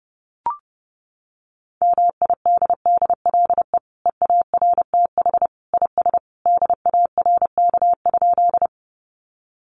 Morse code, heard as a keyed tone, about 28 wpm, 700 Hz. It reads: MIDDLE EARTH IS DARK?
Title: Sentences from Top 900 Words 30wpm